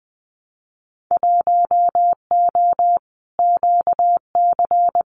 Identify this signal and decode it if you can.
Morse code, heard as a keyed tone, about 20 wpm, 700 Hz. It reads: 1O QC
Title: Winter Field Day Exchanges 20wpm